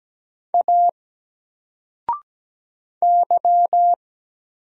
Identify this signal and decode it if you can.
Morse code, heard as a keyed tone, about 17 wpm, 700 Hz. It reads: A Y
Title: Single Letter-Number 17wpm